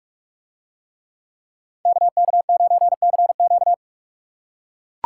Morse code, heard as a keyed tone, about 44 wpm, 700 Hz. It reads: KK9CQ